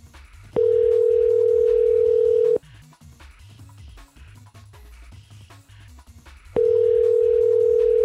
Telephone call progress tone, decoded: ringback tone